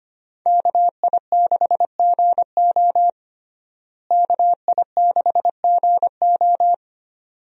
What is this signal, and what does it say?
Morse code, heard as a keyed tone, about 25 wpm, 700 Hz. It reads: KI6GO KI6GO
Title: USA Call Signs - 5 Characters - Repeated 2x - 25wpm